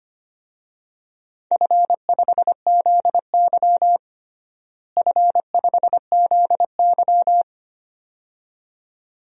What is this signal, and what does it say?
Morse code, heard as a keyed tone, about 25 wpm, 700 Hz. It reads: F5ZY F5ZY
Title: DXCC-100 Call Signs - 4 Characters - Repeated 2x - 25wpm